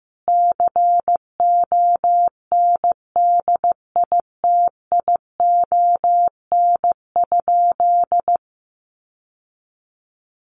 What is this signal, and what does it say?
Morse code, heard as a keyed tone, about 14 wpm, 700 Hz. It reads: CONDITION?
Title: Sentences from Top 1000 Words 15wpm